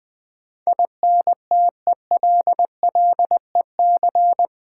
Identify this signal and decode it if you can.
Morse code, heard as a keyed tone, about 20 wpm, 700 Hz. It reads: INTELLEC
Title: Build A Word - 4 Letters At A Time - 12-Letter Words - 20wpm